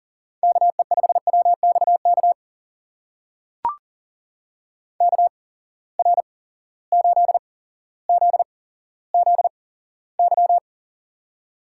Morse code, heard as a keyed tone, about 40 wpm, spaced as for 17 wpm, 700 Hz. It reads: KE5WXK KR8ZZY
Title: USA Call Signs - 6 Characters - 8x Character Spacing - 40wpm